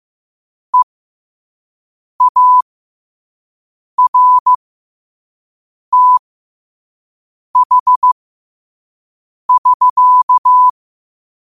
Morse code, heard as a keyed tone, about 15 wpm, spaced as for 5 wpm, 1000 Hz. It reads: EARTH <SK>
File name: qotdfiles / 5wpm